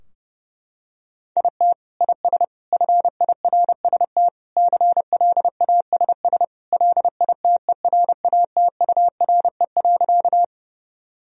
Morse code, heard as a keyed tone, about 29 wpm, 700 Hz. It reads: IT IS FIRST CLASS LITERATURE.
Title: Sentences from Top 1000 Words - Review Part 3 - 30wpm